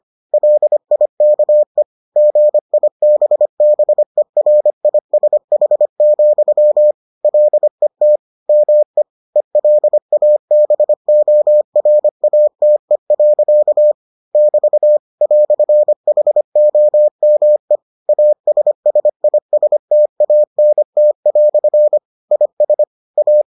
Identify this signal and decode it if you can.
Morse code, heard as a keyed tone, about 25 wpm, 600 Hz. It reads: LIKE GIBBERISH, LET ME ELABORATE. <BT> "HOME ASSISTANT" IS A